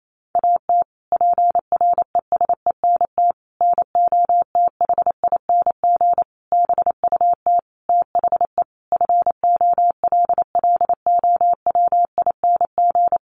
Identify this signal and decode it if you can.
Morse code, heard as a keyed tone, about 28 wpm, 700 Hz. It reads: AT PRESENT NOTHING BUT THE FOLLOWING
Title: Sentences from Top 1500 Words 28wpm